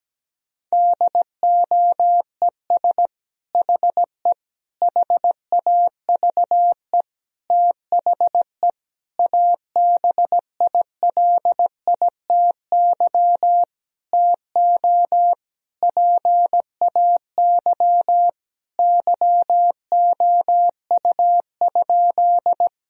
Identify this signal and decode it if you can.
Morse code, heard as a keyed tone, about 17 wpm, 700 Hz. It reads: DOES HE HAVE THE ABILITY TO PAY YOU?
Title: Sentences from Top 1000 Words - Review Part 3 - 17wpm